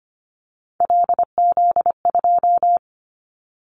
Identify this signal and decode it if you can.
Morse code, heard as a keyed tone, about 25 wpm, 700 Hz. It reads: LZ2
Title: Callsign Prefixes for Top 25 DXCC - 25wpm